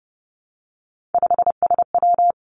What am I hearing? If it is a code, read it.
Morse code, heard as a keyed tone, about 30 wpm, 700 Hz. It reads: 5SW
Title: Three Challenging Letter-Number Combinations 30wpm